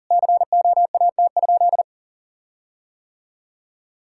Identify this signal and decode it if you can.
Morse code, heard as a keyed tone, about 40 wpm, 700 Hz. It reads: COAT?
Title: Sentences from Top 1600 Words 40wpm